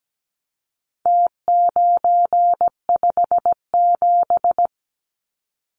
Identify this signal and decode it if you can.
Morse code, heard as a keyed tone, about 17 wpm, 700 Hz. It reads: T957
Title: Numbers - Contest Style - 4-Digit 17wpm